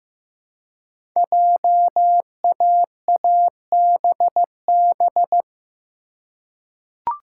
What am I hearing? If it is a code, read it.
Morse code, heard as a keyed tone, about 15 wpm, 700 Hz. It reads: JAABB
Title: Code Groups - 5 Letters - 15wpm